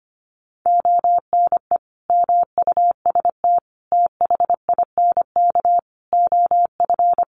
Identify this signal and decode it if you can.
Morse code, heard as a keyed tone, about 25 wpm, 700 Hz. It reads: ONE MUST THINK OF
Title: Sentences from Top 700 Words 25wpm